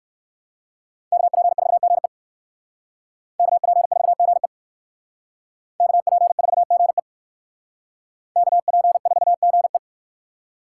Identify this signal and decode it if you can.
Morse code, heard as a keyed tone, about 57 wpm, 700 Hz. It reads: KP4GE KP4GE KP4GE KP4GE